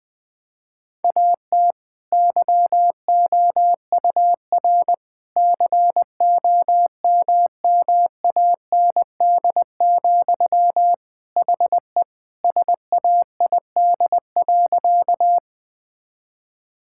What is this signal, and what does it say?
Morse code, heard as a keyed tone, about 20 wpm, 700 Hz. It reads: AT YOUR COMMAND, HE SAID.
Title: Sentences from Top 1600 Words 20wpm